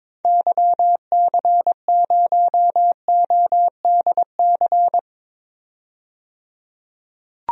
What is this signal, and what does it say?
Morse code, heard as a keyed tone, about 22 wpm, 700 Hz. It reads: YC0ODC
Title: DXCC-100 Call Signs - 6 Characters - 22wpm